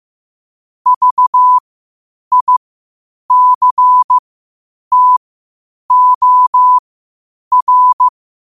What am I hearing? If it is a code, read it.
Morse code, heard as a keyed tone, about 15 wpm, spaced as for 8 wpm, 1000 Hz. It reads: VICTOR
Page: qotd-at-8-wpm